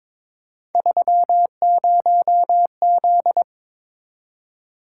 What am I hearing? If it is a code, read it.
Morse code, heard as a keyed tone, about 22 wpm, 700 Hz. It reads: 30Z
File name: practice / 22wpm / Three Letter-Number Combinations - 22wpm